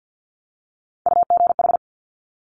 Morse code, heard as a keyed tone, about 50 wpm, 700 Hz. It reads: UGH
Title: Top Three Letter Combinations in Words 50wpm